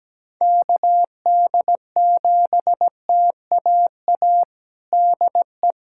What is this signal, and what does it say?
Morse code, heard as a keyed tone, about 17 wpm, 700 Hz. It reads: KD7TAA DE
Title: USA Callsign DE Callsign 17wpm